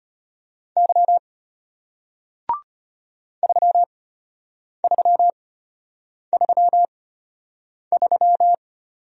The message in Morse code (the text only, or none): Y 3333